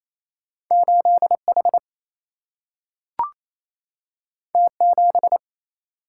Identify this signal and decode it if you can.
Morse code, heard as a keyed tone, about 28 wpm, 700 Hz. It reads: 8 H T 7
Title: Single-Letter Single-Number 28wpm